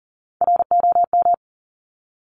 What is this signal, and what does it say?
Morse code, heard as a keyed tone, about 40 wpm, 700 Hz. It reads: ROM
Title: Top Three Letter Combinations in Words 40wpm